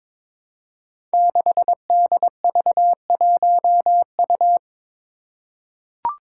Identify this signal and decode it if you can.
Morse code, heard as a keyed tone, about 22 wpm, 700 Hz. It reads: 6DV1U
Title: Five Challenging Letter-Number Combinations 22wpm